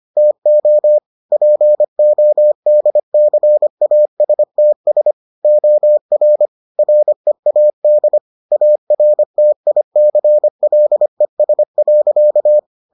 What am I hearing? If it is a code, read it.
Morse code, heard as a keyed tone, about 25 wpm, 600 Hz. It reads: TO PODCASTS OR READ ARTICLES.